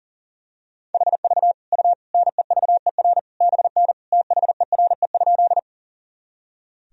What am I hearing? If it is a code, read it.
Morse code, heard as a keyed tone, about 40 wpm, 700 Hz. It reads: HV U NEVER BN THERE?
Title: Sentences from Top 300 Words and Abbreviations 40wpm